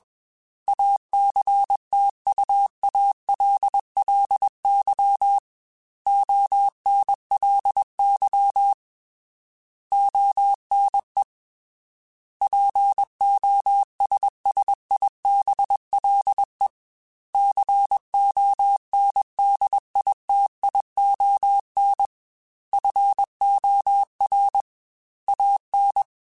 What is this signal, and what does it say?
Morse code, heard as a keyed tone, about 21 wpm, 800 Hz. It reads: ACTUALLY ONLY ONE POSSIBLE CONDITION FOR AN
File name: Technical Subjects / Technical Subjects-mp3s / AND Gate @20